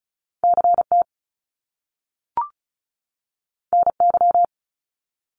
Morse code, heard as a keyed tone, about 36 wpm, 700 Hz. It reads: CT NY
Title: Top Two Letter Combinations in Words 35wpm